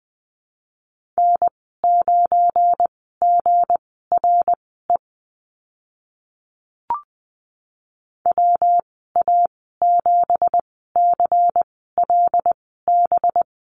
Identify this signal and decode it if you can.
Morse code, heard as a keyed tone, about 20 wpm, spaced as for 14 wpm, 700 Hz. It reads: N9GRE WA7CLB
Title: 193 Lesson 31g - Letter v - Call Signs 20wpm